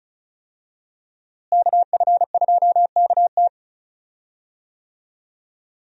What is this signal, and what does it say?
Morse code, heard as a keyed tone, about 34 wpm, 700 Hz. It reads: KF2KT
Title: USA Call Signs - 5 Characters - Repeated 2x - 35wpm